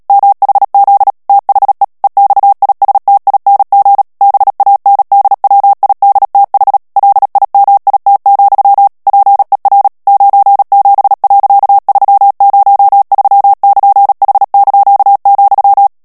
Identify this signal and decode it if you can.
Morse code, heard as a keyed tone, about 37 wpm, 800 Hz. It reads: MHZ THE EXISTING BANDWIDTH LIMIT, PER 97.303<KN>H),
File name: morse mp3 / ANGLAIS texte clair 3 vitesse 35 mots minute